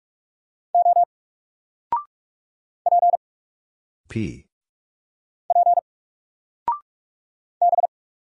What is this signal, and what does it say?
Morse code, heard as a keyed tone, about 44 wpm, 700 Hz. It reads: O P P B
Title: Single Letter-Number 45wpm